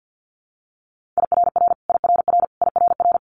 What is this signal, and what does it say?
Morse code, heard as a keyed tone, about 50 wpm, 700 Hz. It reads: IRR IRR IRR
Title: Multi-letter Phonemes Repeated 3x 50wpm